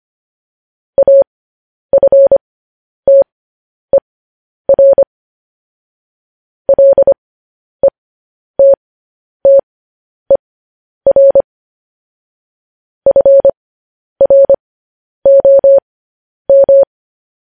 Morse code, heard as a keyed tone, about 25 wpm, spaced as for 10 wpm, 550 Hz. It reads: AFTER LETTER FROM
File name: podcast / News headlines-1x-25x10